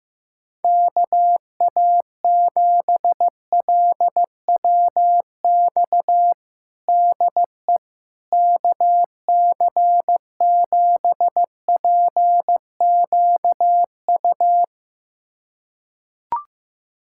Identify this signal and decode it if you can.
Morse code, heard as a keyed tone, about 15 wpm, 700 Hz. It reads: KA7LWX DE KC7PQU
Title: USA Callsign DE Callsign - 6 Characters - 15wpm